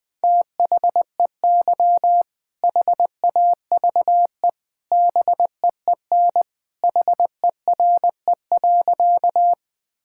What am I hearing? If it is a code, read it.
Morse code, heard as a keyed tone, about 20 wpm, 700 Hz. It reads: THEY HAVE BEEN HERE.
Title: Sentences with 2-Word Phrases - 20wpm